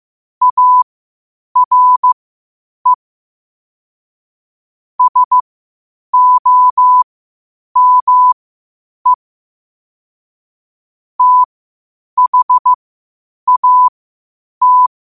Morse code, heard as a keyed tone, about 15 wpm, spaced as for 8 wpm, 1000 Hz. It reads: ARE SOME THAT
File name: qotdfiles / 8wpm